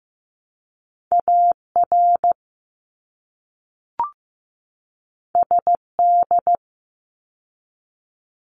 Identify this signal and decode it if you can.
Morse code, heard as a keyed tone, about 15 wpm, 700 Hz. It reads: AR SD
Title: ARRL Sections 15wpm